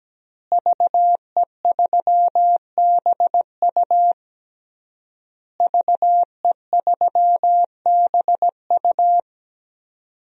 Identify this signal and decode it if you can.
Morse code, heard as a keyed tone, about 17 wpm, 700 Hz. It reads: VE3BU VE3BU
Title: DXCC-100 Call Signs - 5 Characters - Repeated 3x - 17wpm